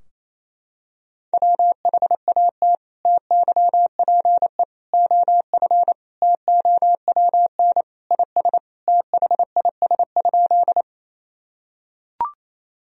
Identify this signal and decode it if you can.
Morse code, heard as a keyed tone, about 28 wpm, 700 Hz. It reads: WHAT TYPE OF TOWN IS THIS?
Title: Sentences from Top 1000 Words - Review Part 3 - 28wpm